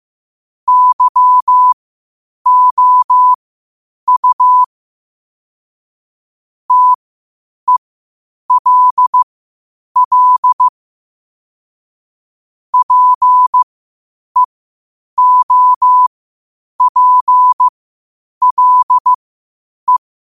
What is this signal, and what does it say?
Morse code, heard as a keyed tone, about 15 wpm, spaced as for 8 wpm, 1000 Hz. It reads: YOU TELL PEOPLE
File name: qotdfiles / 8wpm